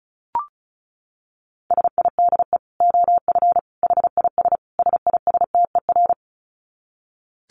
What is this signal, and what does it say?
Morse code, heard as a keyed tone, about 34 wpm, 700 Hz. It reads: SIDE OF HIS SISTER
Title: Dit Challenge 35wpm